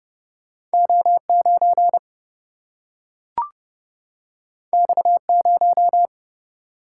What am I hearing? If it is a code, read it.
Morse code, heard as a keyed tone, about 30 wpm, 700 Hz. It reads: O9 X0
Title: Single-Letter Single-Number 30wpm